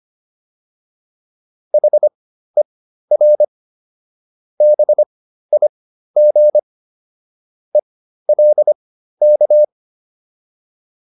Morse code, heard as a keyed tone, about 25 wpm, spaced as for 13 wpm, 600 Hz. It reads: HER BIG ELK
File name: PR202 13